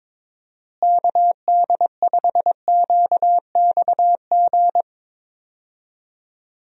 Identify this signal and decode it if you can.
Morse code, heard as a keyed tone, about 22 wpm, 700 Hz. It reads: KD5QXG
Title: USA Call Signs - 6 Characters - Repeated 3x - 22wpm